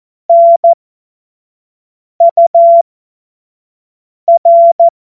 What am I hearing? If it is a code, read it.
Morse code, heard as a keyed tone, about 14 wpm, spaced as for 5 wpm, 680 Hz. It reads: NUR
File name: Witz-23-September-2025-25-F70